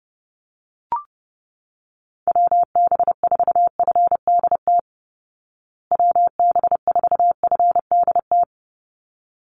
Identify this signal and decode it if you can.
Morse code, heard as a keyed tone, about 30 wpm, 700 Hz. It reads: WB4FDT WB4FDT
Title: Call Signs Repeated 3x - 30wpm